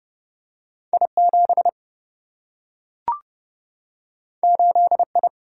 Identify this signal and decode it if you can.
Morse code, heard as a keyed tone, about 30 wpm, 700 Hz. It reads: I 7 8 I
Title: Single-Letter Single-Number 30wpm